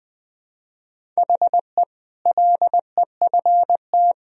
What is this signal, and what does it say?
Morse code, heard as a keyed tone, about 21 wpm, 700 Hz. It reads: HE LEFT
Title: Sentences from Top 1000 Words - Review Part 1 - 20wpm